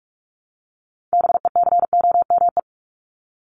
Morse code, heard as a keyed tone, about 44 wpm, 700 Hz. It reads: BECOME
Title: ING Suffix - Top 100 Words with Base Words 45wpm